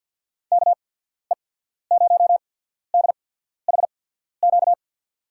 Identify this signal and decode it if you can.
Morse code, heard as a keyed tone, about 50 wpm, spaced as for 14 wpm, 700 Hz. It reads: KE0DHQ
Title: USA Call Signs - 6 Characters - 8x Character Spacing - 50wpm